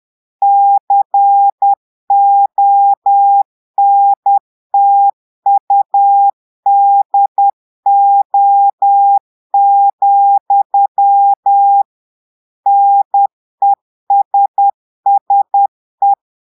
Morse code, heard as a keyed tone, about 10 wpm, 800 Hz. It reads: CONTUDO, NESSE
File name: portugais texte en clair 2 vitesse 10 mots minutes